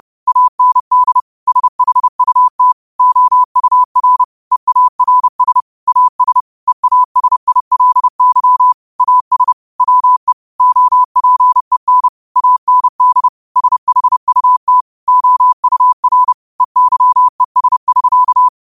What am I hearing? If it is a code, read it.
Morse code, heard as a keyed tone, about 30 wpm, 1000 Hz. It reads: AND SHUT OUR EARS AS EASILY AS WE OPEN AND SHUT OUR EYES <SK>
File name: qotdfiles / 30wpm